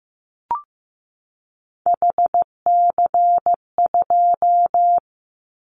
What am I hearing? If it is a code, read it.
Morse code, heard as a keyed tone, about 15 wpm, 700 Hz. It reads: HC2
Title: Callsign Prefixes for Top 100 DXCC - 15wpm